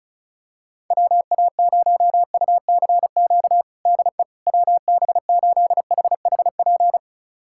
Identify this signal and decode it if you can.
Morse code, heard as a keyed tone, about 34 wpm, 700 Hz. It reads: WA0UCQ DE WB8HHP
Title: USA Callsign DE Callsign - 6 Characters - 35wpm